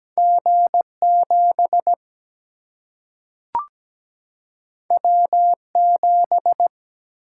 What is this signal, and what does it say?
Morse code, heard as a keyed tone, about 17 wpm, 700 Hz. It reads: G 7 W 7
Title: Single-Letter Single-Number 17wpm